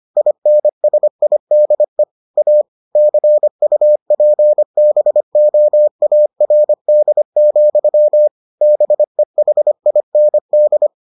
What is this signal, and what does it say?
Morse code, heard as a keyed tone, about 25 wpm, 600 Hz. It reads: INSIDE A CUPBOARD, BEHIND